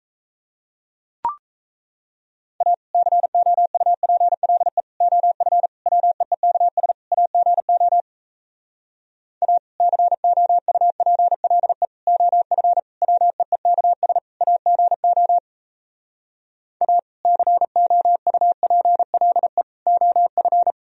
Morse code, heard as a keyed tone, about 37 wpm, 700 Hz. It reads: A COUPLE OF WEEKS AGO A COUPLE OF WEEKS AGO A COUPLE OF